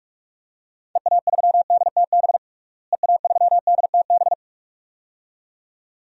Morse code, heard as a keyed tone, about 42 wpm, 700 Hz. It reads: EA3DTB EA3DTB
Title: DXCC-100 Call Signs - 6 Characters - Repeated 2x - 45wpm